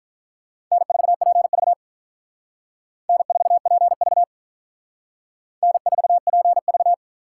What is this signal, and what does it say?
Morse code, heard as a keyed tone, about 45 wpm, 700 Hz. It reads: N4PV N4PV N4PV